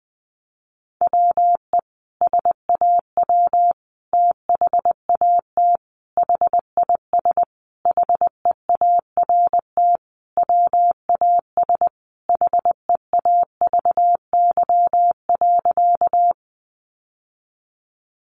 Morse code, heard as a keyed tone, about 20 wpm, 700 Hz. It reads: WE SAW THAT HIS HEART WAS HEAVY.
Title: Sentences from Top 1200 Words 20wpm